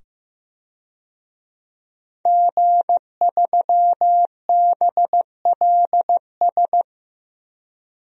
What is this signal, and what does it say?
Morse code, heard as a keyed tone, about 15 wpm, 700 Hz. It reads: G3BLS
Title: DXCC-100 Call Signs - 5 Characters - 15wpm